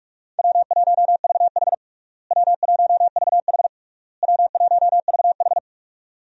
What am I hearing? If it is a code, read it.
Morse code, heard as a keyed tone, about 46 wpm, 700 Hz. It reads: W1VH W1VH W1VH
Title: USA Call Signs - 4 Characters - Repeated 3x - 45wpm